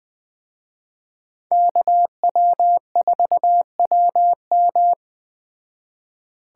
Morse code, heard as a keyed tone, about 21 wpm, 700 Hz. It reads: KW4WM